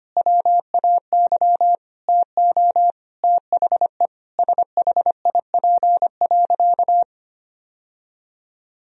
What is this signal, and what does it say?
Morse code, heard as a keyed tone, about 25 wpm, 700 Hz. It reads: WAY TO THE SHIP.
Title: Sentences from Top 1000 Words - Review Part 2 - 25wpm